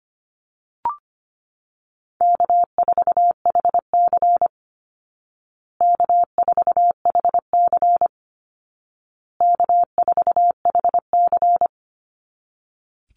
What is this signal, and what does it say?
Morse code, heard as a keyed tone, about 25 wpm, 700 Hz. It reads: K4HC K4HC K4HC
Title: Easy Call Signs Repeated 3x - 25wpm